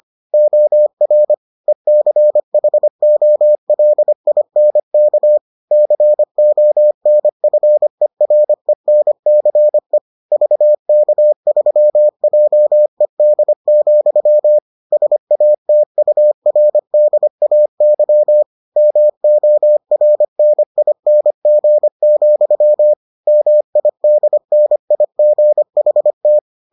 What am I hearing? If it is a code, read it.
Morse code, heard as a keyed tone, about 25 wpm, 600 Hz. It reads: OR ECHOLINK CONFERENCE VK3JED, SATURDAY MORNING, MIDNIGHT